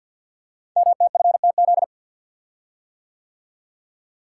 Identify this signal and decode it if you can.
Morse code, heard as a keyed tone, about 52 wpm, 700 Hz. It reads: MTFTZ
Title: Code Groups - 5 Letter:Numbers - 50wpm